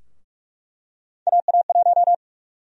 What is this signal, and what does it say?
Morse code, heard as a keyed tone, about 47 wpm, 700 Hz. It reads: AA1
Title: USA Call Sign Prefixes - Repeated 2x - 45wpm